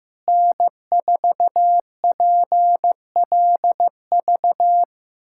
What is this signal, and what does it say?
Morse code, heard as a keyed tone, about 16 wpm, 700 Hz. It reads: N4PLV
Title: USA Callsign DE Callsign - 5 Characters - 15wpm